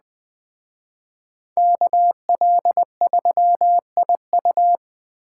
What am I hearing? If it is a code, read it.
Morse code, heard as a keyed tone, about 20 wpm, 700 Hz. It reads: KL3IU
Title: Speed Racing - USA Call Signs 17wpm